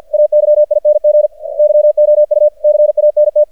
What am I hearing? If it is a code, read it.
Morse code, heard as a keyed tone, about 49 wpm, 600 Hz. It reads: AYING YOU CANT
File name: archived / CRSnet